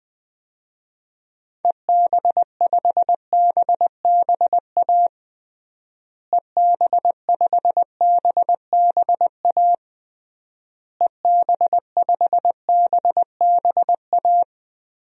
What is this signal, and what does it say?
Morse code, heard as a keyed tone, about 20 wpm, 700 Hz. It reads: EB5BBA EB5BBA EB5BBA
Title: DXCC-100 Call Signs - 6 Characters - Repeated 3x - 20wpm